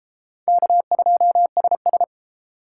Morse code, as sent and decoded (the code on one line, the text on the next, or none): -.- ..--- ... ...
K2SS